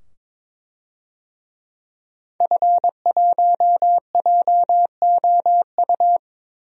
Morse code, heard as a keyed tone, about 21 wpm, 700 Hz. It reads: F1JOU
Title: DXCC-100 Call Signs - 5 Characters - 22wpm